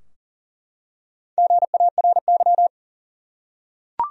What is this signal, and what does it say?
Morse code, heard as a keyed tone, about 40 wpm, 700 Hz. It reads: GARY
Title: USA Common Names - Max 4 Letters - 40wpm